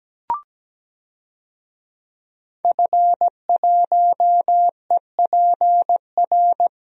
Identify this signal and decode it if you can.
Morse code, heard as a keyed tone, about 17 wpm, 700 Hz. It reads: F1EPR
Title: DXCC-100 Call Signs - 5 Characters - Repeated 3x - 17wpm